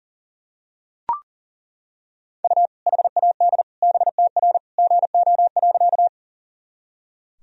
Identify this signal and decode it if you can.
Morse code, heard as a keyed tone, about 40 wpm, 700 Hz. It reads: U HAD BTR GO.
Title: Sentences from Top 400 Words and Abbreviations 40wpm